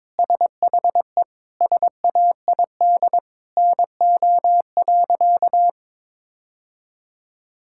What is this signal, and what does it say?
Morse code, heard as a keyed tone, about 22 wpm, 700 Hz. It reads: SHE SAID NO.
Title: Sentences from Top 1000 Words - Review Part 6 - 22wpm